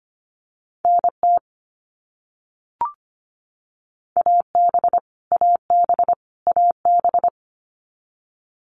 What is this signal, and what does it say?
Morse code, heard as a keyed tone, about 25 wpm, 700 Hz. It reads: NT AB AB AB
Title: Canadian Provinces and Territories - Repeated 3x - 25wpm